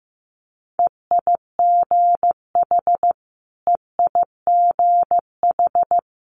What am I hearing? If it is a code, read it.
Morse code, heard as a keyed tone, about 15 wpm, 700 Hz. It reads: EIGH EIGH
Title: Multi-letter Phonemes Repeated 3x 15wpm